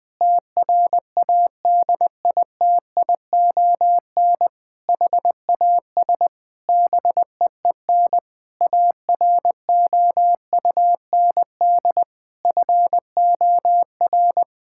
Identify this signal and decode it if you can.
Morse code, heard as a keyed tone, about 20 wpm, 700 Hz. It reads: TRADITION HAS BEEN AROUND FOR
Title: Sentences with 7-Word Phrases - 20wpm